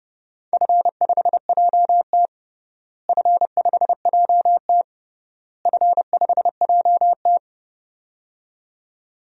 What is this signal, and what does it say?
Morse code, heard as a keyed tone, about 32 wpm, 700 Hz. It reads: F5JT F5JT F5JT
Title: DXCC-100 Call Signs - 4 Characters - Repeated 3x - 30wpm